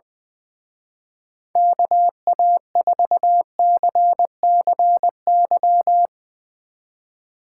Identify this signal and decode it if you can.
Morse code, heard as a keyed tone, about 20 wpm, 700 Hz. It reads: KA4CCY
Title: Speed Racing - USA 2x3 Call Signs 20wpm